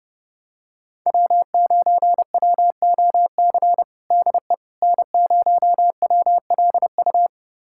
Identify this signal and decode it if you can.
Morse code, heard as a keyed tone, about 31 wpm, 700 Hz. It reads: W9WOC DE N0WLU
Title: USA Callsign DE Callsign - 5 Characters - 30wpm